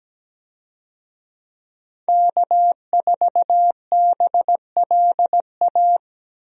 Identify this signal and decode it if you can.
Morse code, heard as a keyed tone, about 17 wpm, 700 Hz. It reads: K4BLA
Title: USA Call Signs - 5 Characters - Repeated 3x - 17wpm